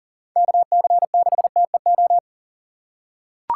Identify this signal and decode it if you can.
Morse code, heard as a keyed tone, about 37 wpm, 700 Hz. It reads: KC6TEO
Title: USA Call Signs - 6 Characters - 6x Character Spacing - 40wpm